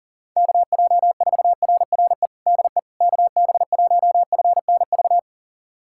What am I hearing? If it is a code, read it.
Morse code, heard as a keyed tone, about 40 wpm, 700 Hz. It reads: KJ4RRE DE KB1FNV
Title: USA Callsign DE Callsign - 6 Characters - 40wpm